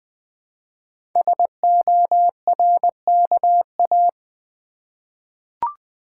Morse code, heard as a keyed tone, about 20 wpm, 700 Hz. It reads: SORKA
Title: Code Groups - 5 Letters - 20wpm